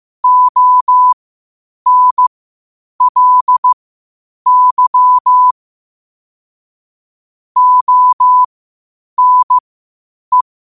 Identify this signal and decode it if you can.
Morse code, heard as a keyed tone, about 15 wpm, spaced as for 8 wpm, 1000 Hz. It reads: ONLY ONE
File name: qotdfiles / 8wpm